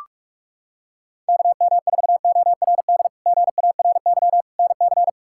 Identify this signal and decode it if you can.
Morse code, heard as a keyed tone, about 46 wpm, 700 Hz. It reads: KM4ORD GARY NC
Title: K1USN SST QSO Practice - 45wpm